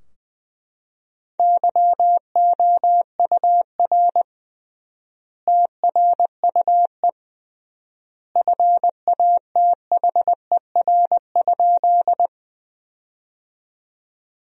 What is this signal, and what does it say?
Morse code, heard as a keyed tone, about 20 wpm, 700 Hz. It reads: YOUR TRUE FATHER?
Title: 3x Word-Spacing - Sentences from Top 400 Words - 20wpm